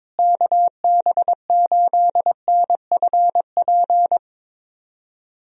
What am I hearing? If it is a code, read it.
Morse code, heard as a keyed tone, about 22 wpm, 700 Hz. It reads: KB8NFP
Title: USA Call Signs - 6 Characters - Repeated 2x - 22wpm